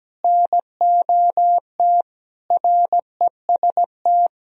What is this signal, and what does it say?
Morse code, heard as a keyed tone, about 17 wpm, 700 Hz. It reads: NOT REST
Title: Sentences from Top 1000 Words - Review Part 1 - 17wpm